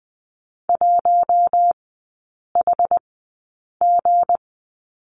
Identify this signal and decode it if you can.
Morse code, heard as a keyed tone, about 20 wpm, spaced as for 8 wpm, 700 Hz. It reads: 1HG
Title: 166 Lesson 27e - Number 7 - Three Characters 20wpm